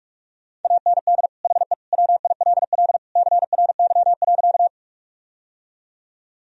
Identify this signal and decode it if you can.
Morse code, heard as a keyed tone, about 41 wpm, 700 Hz. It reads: AND HE WILL CRY.
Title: Sentences from Top 1000 Words - Review Part 2 - 45wpm